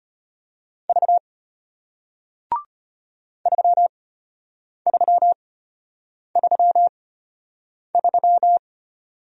Morse code, heard as a keyed tone, about 33 wpm, spaced as for 8 wpm, 690 Hz. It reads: V 3333